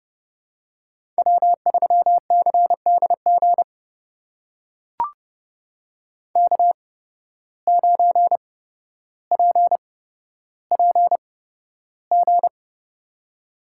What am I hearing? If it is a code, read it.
Morse code, heard as a keyed tone, about 30 wpm, spaced as for 13 wpm, 700 Hz. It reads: W3CDG K9PPG
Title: USA Call Signs - 5 Characters - 8x Character Spacing - 30wpm